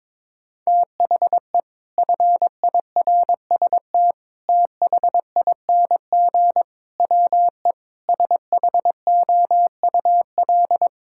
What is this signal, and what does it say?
Morse code, heard as a keyed tone, about 22 wpm, 700 Hz. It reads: THE FIRST THING WE SHOUL
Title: Sentences from Top 1000 Words - Review Part 6 - 22wpm